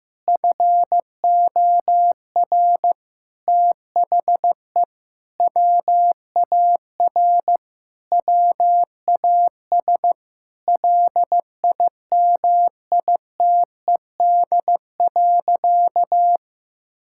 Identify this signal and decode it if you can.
Morse code, heard as a keyed tone, about 15 wpm, 700 Hz. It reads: FOR THE WAR WAS LIMITED.
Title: Sentences from Top 1000 Words - Review Part 6 - 15wpm